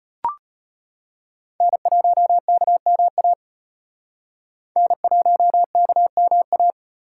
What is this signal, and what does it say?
Morse code, heard as a keyed tone, about 33 wpm, 700 Hz. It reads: N1KMA N1KMA